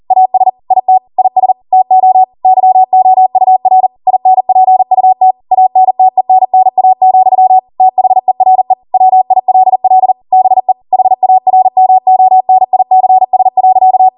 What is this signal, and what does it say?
Morse code, heard as a keyed tone, about 40 wpm, 750 Hz. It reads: AS IT IS TO YOUR INPUT ANTENNA, THERE WILL BE HARMONICS.
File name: Morse / 40WPM